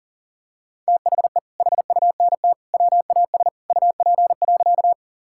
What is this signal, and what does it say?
Morse code, heard as a keyed tone, about 40 wpm, 700 Hz. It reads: THE HUNT WAS UP.
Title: Sentences from Top 1600 Words 40wpm